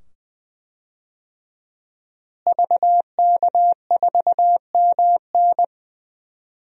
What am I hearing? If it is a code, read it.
Morse code, heard as a keyed tone, about 21 wpm, 700 Hz. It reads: VK4MN
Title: DXCC-100 Call Signs - 5 Characters - Repeated 2x - 20wpm